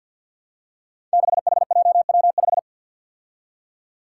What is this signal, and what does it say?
Morse code, heard as a keyed tone, about 53 wpm, 700 Hz. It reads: 6HJW5
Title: Five Challenging Letter-Number Combinations 50wpm